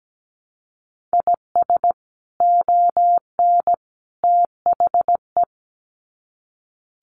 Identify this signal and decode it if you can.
Morse code, heard as a keyed tone, about 17 wpm, 700 Hz. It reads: IS ON THE
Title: Sets of 3 Words 17wpm